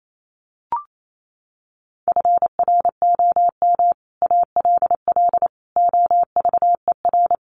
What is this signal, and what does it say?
Morse code, heard as a keyed tone, about 28 wpm, 700 Hz. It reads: FROM ALL OVER
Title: Sets of 5 Words Encore 28wpm